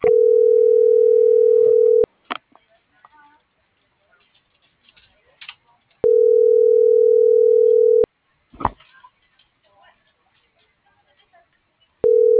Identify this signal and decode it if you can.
Ringback tone (North America), 440 + 480 Hz.